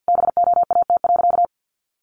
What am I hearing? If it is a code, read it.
Morse code, heard as a keyed tone, about 50 wpm, 700 Hz. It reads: BOAT.
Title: Sentences from Top 1400 Words 50wpm